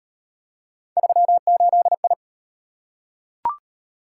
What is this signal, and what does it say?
Morse code, heard as a keyed tone, about 38 wpm, 700 Hz. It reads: 38I